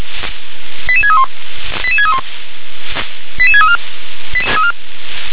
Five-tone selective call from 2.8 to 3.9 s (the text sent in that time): 88634